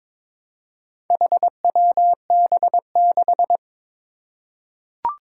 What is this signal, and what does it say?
Morse code, heard as a keyed tone, about 22 wpm, 700 Hz. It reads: HWB6
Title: Four Challenging Letter-Number Combinations 22wpm